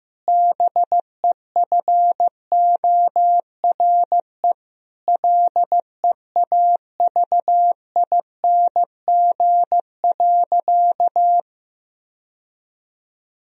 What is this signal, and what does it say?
Morse code, heard as a keyed tone, about 15 wpm, 700 Hz. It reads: BEFORE LEAVING.
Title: Sentences from Top 1000 Words - Review Part 4 - 15wpm